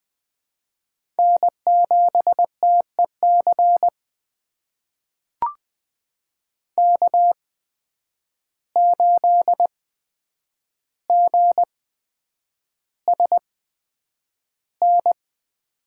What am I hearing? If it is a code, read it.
Morse code, heard as a keyed tone, about 20 wpm, spaced as for 9 wpm, 700 Hz. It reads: N7TEC K8GSN
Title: USA Call Signs - 5 Characters - 8x Character Spacing - 20wpm